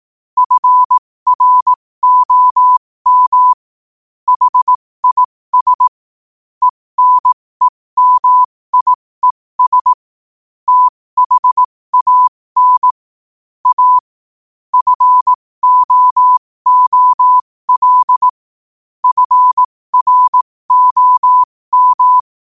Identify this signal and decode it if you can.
Morse code, heard as a keyed tone, about 18 wpm, spaced as for 15 wpm, 1000 Hz. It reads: FROM HIS ENEMIES THAN A FOOL FROM
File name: qotdfiles / 15wpm